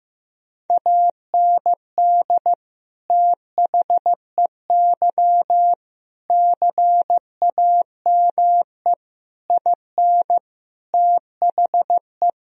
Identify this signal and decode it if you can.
Morse code, heard as a keyed tone, about 15 wpm, 700 Hz. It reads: AND THEY CAME IN THE
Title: Sentences from Top 1000 Words - Review Part 1 - 15wpm